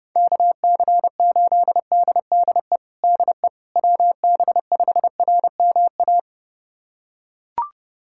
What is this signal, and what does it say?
Morse code, heard as a keyed tone, about 30 wpm, 700 Hz. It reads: KC8DDE DE WB5RMA
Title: USA Callsign DE Callsign - 6 Characters - 30wpm